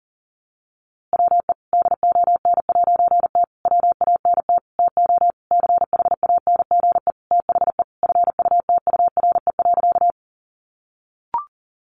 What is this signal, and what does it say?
Morse code, heard as a keyed tone, about 40 wpm, 700 Hz. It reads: WE DON'T WANT TO CHANGE THE FUTURE.
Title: A Sound Of Thunder 40wpm